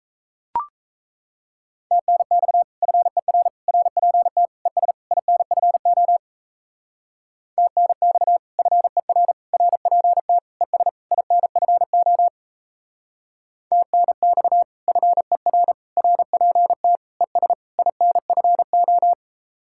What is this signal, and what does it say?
Morse code, heard as a keyed tone, about 35 wpm, 700 Hz. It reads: TNX FER RPT ES INFO TNX FER RPT ES INFO TNX FER RPT ES INFO
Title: Speed-Racing Common QSO Phrases - 28wpm